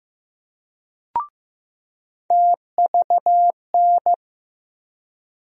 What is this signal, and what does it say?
Morse code, heard as a keyed tone, about 15 wpm, 700 Hz. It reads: TVN
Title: USA Call Sign Postfixes - 3 Letters 15wpm